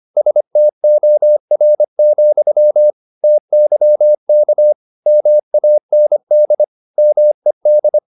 Morse code, heard as a keyed tone, about 25 wpm, 600 Hz. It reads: STOR, TYK MAND MED
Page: Ferie 24 wpm